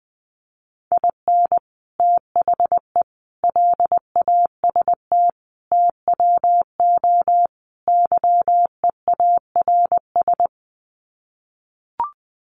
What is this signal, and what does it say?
Morse code, heard as a keyed tone, about 20 wpm, 700 Hz. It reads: IN THE LAST TWO YEARS
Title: Sets of 5 Words 20wpm